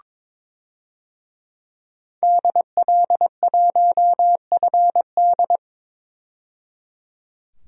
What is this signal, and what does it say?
Morse code, heard as a keyed tone, about 21 wpm, 700 Hz. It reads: DL1FD
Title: DXCC-100 Call Signs - 5 Characters - 22wpm